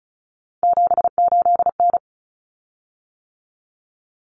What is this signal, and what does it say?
Morse code, heard as a keyed tone, about 34 wpm, 700 Hz. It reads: 78N
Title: Common Cut Numbers - 3-Digit 35wpm